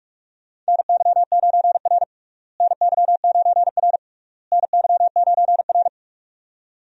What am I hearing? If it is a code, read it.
Morse code, heard as a keyed tone, about 46 wpm, 700 Hz. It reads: NY9R NY9R NY9R
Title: USA Call Signs - 4 Characters - Repeated 3x - 45wpm